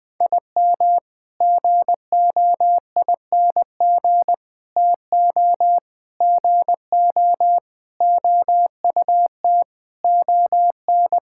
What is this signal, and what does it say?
Morse code, heard as a keyed tone, about 21 wpm, 700 Hz. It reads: IM GOING TO GO OUT ON
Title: Sentences with 8-Word Phrases - 20wpm